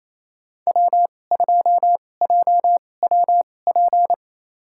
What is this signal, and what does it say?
Morse code, heard as a keyed tone, about 28 wpm, spaced as for 20 wpm, 700 Hz. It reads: W2JWP
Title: USA Call Signs - 5 Characters - 2x Character Spacing - 28wpm